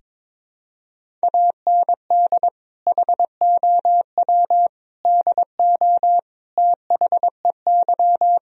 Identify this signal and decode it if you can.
Morse code, heard as a keyed tone, about 22 wpm, 700 Hz. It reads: AND HOW DO THEY
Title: Sentences from Top 1000 Words - Review Part 4 - 22wpm